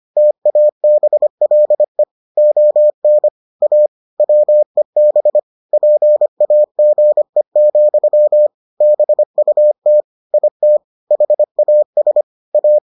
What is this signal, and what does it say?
Morse code, heard as a keyed tone, about 25 wpm, 600 Hz. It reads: TABLE ON A WEB PAGE, BUT IT HAS A